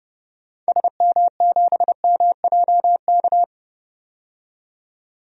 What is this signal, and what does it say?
Morse code, heard as a keyed tone, about 30 wpm, 700 Hz. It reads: SM7MJK
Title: DXCC-100 Call Signs - 6 Characters - 30wpm